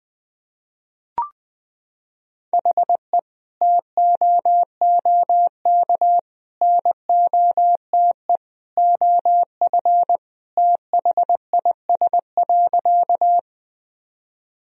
Morse code, heard as a keyed tone, about 20 wpm, 700 Hz. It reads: HE TOOK NOTE OF THIS.
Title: Sentences from Top 1000 Words - Review Part 2 - 20wpm